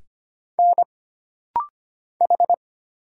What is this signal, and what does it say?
Morse code, heard as a keyed tone, about 25 wpm, 700 Hz. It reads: N H
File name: Single Character (Letter:Number:Punctuation:Prosign) - Rapid-Fire - 25wpm